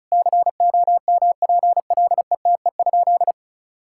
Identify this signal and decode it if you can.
Morse code, heard as a keyed tone, about 34 wpm, 690 Hz. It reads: COMPLETE?
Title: Sentences from Top 1000 Words - Review Part 6 - 35wpm